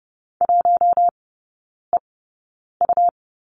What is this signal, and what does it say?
Morse code, heard as a keyed tone, about 30 wpm, spaced as for 9 wpm, 700 Hz. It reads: 1EU
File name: Lesson 18e - Letter m - Three Characters 30wpm